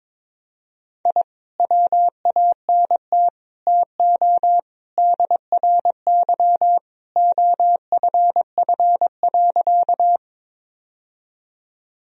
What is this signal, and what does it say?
Morse code, heard as a keyed tone, about 22 wpm, 700 Hz. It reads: I WANT TO DRY OFF.
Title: Sentences from Top 1000 Words - Review Part 3 - 22wpm